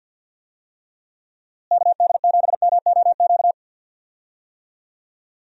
Morse code, heard as a keyed tone, about 51 wpm, 700 Hz. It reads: KD7MOQ
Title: USA Call Signs - 6 Characters - Repeated 3x - 50wpm